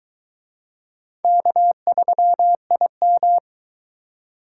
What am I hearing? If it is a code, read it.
Morse code, heard as a keyed tone, about 21 wpm, 700 Hz. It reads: K3IM